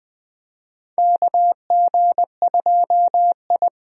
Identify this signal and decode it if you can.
Morse code, heard as a keyed tone, about 21 wpm, 700 Hz. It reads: KG2I
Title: USA Call Signs - 4 Characters - 20wpm